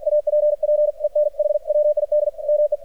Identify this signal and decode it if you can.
Morse code, heard as a keyed tone, about 53 wpm, 600 Hz, with radio noise. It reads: APPEN H WID ME
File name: archived / CRSnet